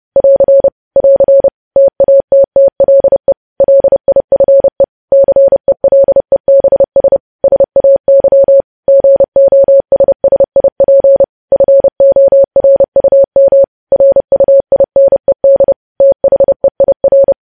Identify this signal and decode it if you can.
Morse code, heard as a keyed tone, about 30 wpm, 550 Hz. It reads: <AR> <AR> TATTLE LIFE CELEBS SAY GOSSIP FORUM RUINED THEIR
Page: news headlines in morse code at 30 wpm